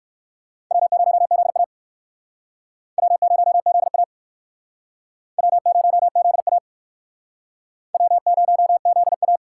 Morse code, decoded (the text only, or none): W0ZA W0ZA W0ZA W0ZA